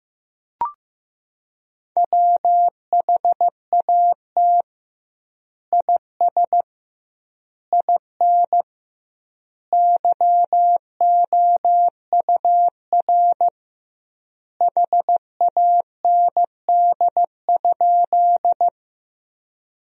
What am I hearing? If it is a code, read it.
Morse code, heard as a keyed tone, about 15 wpm, 700 Hz. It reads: WHAT IS IN YOUR HAND?
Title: 2x Word-Spacing - Sentences from Top 200 Words - 15wpm